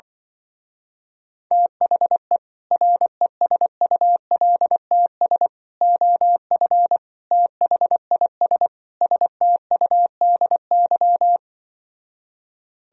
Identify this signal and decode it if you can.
Morse code, heard as a keyed tone, about 24 wpm, 700 Hz. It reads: THE RESULTS OF THIS STUDY
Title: Speed-Racing - Sets of 5 Words 20wpm